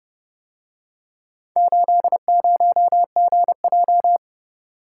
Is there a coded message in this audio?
Yes: Morse code from 1.6 to 4.2 s, heard as a keyed tone, about 29 wpm, 700 Hz. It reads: 80GJ